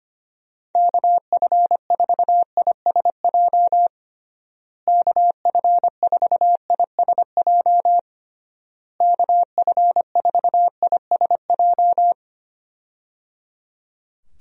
Morse code, heard as a keyed tone, about 25 wpm, 700 Hz. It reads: KF4ISJ KF4ISJ KF4ISJ
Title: USA Call Signs - 6 Characters - Repeated 3x - 25wpm